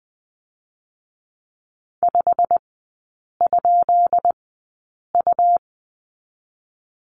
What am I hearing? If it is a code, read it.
Morse code, heard as a keyed tone, about 20 wpm, spaced as for 8 wpm, 700 Hz. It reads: 5?U
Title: 198 Lesson 32e - Letter k - Three Characters 20wpm